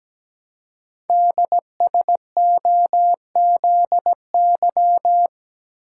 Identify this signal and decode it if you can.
Morse code, heard as a keyed tone, about 17 wpm, 700 Hz. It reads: DSOZY
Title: Code Groups - 5 Letters - 17wpm